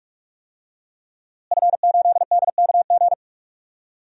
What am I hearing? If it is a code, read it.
Morse code, heard as a keyed tone, about 46 wpm, 700 Hz. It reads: F8DKG
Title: DXCC-100 Call Signs - 5 Characters - Repeated 2x - 45wpm